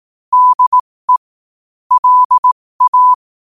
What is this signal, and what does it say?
Morse code, heard as a keyed tone, about 18 wpm, spaced as for 15 wpm, 1000 Hz. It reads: DE LA
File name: qotdfiles / 15wpm